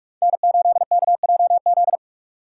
Morse code, heard as a keyed tone, about 44 wpm, 700 Hz. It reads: N8KJZ
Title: Code Groups - 5 Letter:Numbers - 45wpm